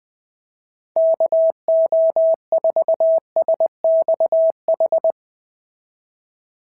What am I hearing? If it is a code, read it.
Morse code, heard as a keyed tone, about 18 wpm, 650 Hz. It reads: KO4SXH
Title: POTA QSO Practice - 20wpm